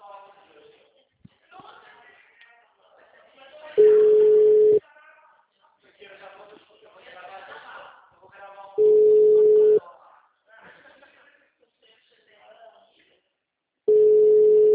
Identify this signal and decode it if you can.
Ringback tone (Europe, 425 Hz), with speech.